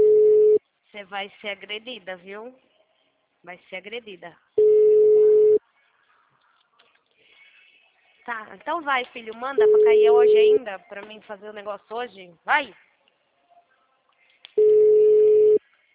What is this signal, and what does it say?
Ringback tone (Europe, 425 Hz), with speech.
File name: out- -2028- - - .115